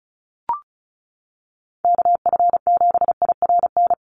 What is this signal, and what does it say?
Morse code, heard as a keyed tone, about 34 wpm, 700 Hz. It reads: KF7IRN
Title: USA Call Signs Repeated 2x 35wpm